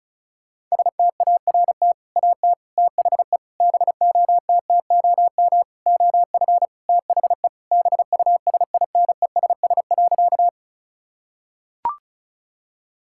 Morse code, heard as a keyed tone, about 34 wpm, 700 Hz. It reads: START AT THE BOTTOM OF THE BUSINESS.
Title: Sentences from Top 1000 Words - Review Part 6 - 35wpm